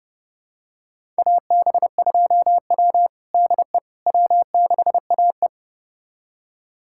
Morse code, heard as a keyed tone, about 30 wpm, 700 Hz. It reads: AB2W DE W6AE
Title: USA Callsign DE Callsign - 4 Characters - 30wpm